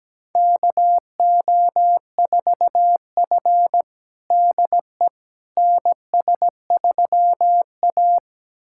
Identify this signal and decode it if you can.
Morse code, heard as a keyed tone, about 17 wpm, 700 Hz. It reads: KO4F DE NS3A
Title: USA Callsign DE Callsign - 4 Characters - 17wpm